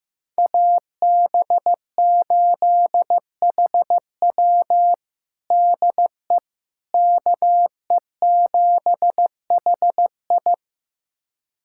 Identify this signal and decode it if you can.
Morse code, heard as a keyed tone, about 15 wpm, 700 Hz. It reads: AB8HW DE KE7HI
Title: USA Callsign DE Callsign - 5 Characters - 15wpm